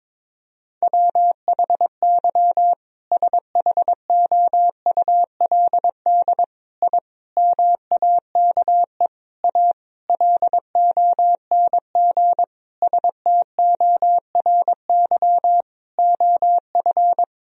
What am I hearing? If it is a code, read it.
Morse code, heard as a keyed tone, about 22 wpm, 700 Hz. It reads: WHY SHOULD I MAKE A LONG STORY OF
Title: Sentences from Top 1000 Words - Review Part 1 - 22wpm